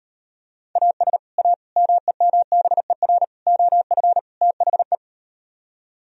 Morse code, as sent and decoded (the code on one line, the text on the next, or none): .- ... / .- / -- . -- -... .-. / --- ..-. / - .... .
AS A MEMBER OF THE